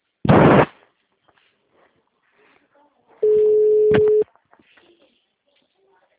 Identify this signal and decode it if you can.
Ringback tone (Europe, 425 Hz), with speech.